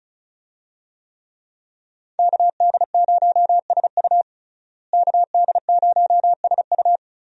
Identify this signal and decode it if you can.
Morse code, heard as a keyed tone, about 34 wpm, 700 Hz. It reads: KD0SU KD0SU
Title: USA Call Signs - 5 Characters - Repeated 2x - 35wpm